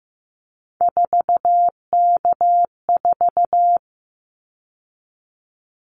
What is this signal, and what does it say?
Morse code, heard as a keyed tone, about 15 wpm, 700 Hz. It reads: 4K4
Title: Callsign Prefixes for Top 100 DXCC - 15wpm